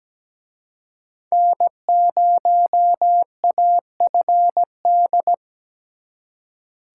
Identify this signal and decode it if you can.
Morse code, heard as a keyed tone, about 20 wpm, 700 Hz. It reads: N0AFD
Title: Speed Racing - USA 1x3 Call Signs 17wpm